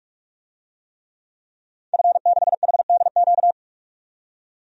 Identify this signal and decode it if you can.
Morse code, heard as a keyed tone, about 46 wpm, 700 Hz. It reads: F6HDQ
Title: DXCC-100 Call Signs - 5 Characters - 45wpm